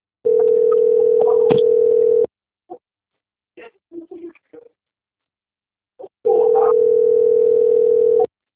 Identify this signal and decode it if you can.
Ringback tone (North America), 440 + 480 Hz.
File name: out- - - - - .60586